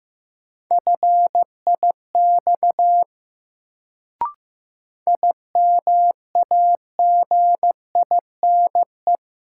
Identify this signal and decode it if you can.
Morse code, heard as a keyed tone, about 15 wpm, 700 Hz. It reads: FIX IMAGINE
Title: Sentences from Top 1400 Words - New Word Drill - 15wpm